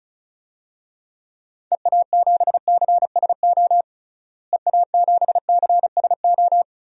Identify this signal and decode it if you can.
Morse code, heard as a keyed tone, about 34 wpm, 700 Hz. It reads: EA7CSO EA7CSO
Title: DXCC-100 Call Signs - 6 Characters - Repeated 3x - 35wpm